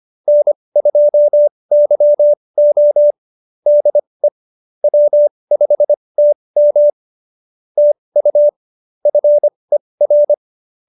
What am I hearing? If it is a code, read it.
Morse code, heard as a keyed tone, about 25 wpm, spaced as for 20 wpm, 590 Hz. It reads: N2YO DE W5TM TU FER
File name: QSO316 20